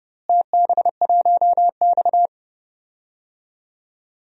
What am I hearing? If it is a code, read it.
Morse code, heard as a keyed tone, about 30 wpm, 700 Hz. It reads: TB1X
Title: DXCC-100 Call Signs - 4 Characters - Repeated 2x - 30wpm